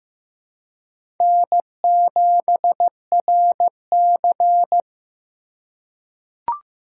Morse code, heard as a keyed tone, about 15 wpm, 700 Hz. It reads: N7RC
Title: USA Call Signs - 4 Characters - 3x Character Spacing - 15wpm